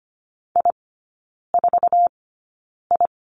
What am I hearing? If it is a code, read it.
Morse code, heard as a keyed tone, about 25 wpm, spaced as for 9 wpm, 700 Hz. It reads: I4I
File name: course / 25:10wpm / 093 Lesson 16e - Letter c - Three Characters 25wpm